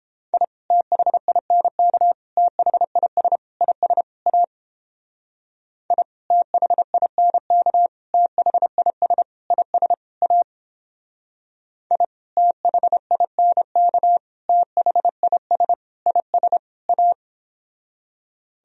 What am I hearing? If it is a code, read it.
Morse code, heard as a keyed tone, about 29 wpm, 700 Hz. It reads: I THINK THIS IS A I THINK THIS IS A I THINK THIS IS A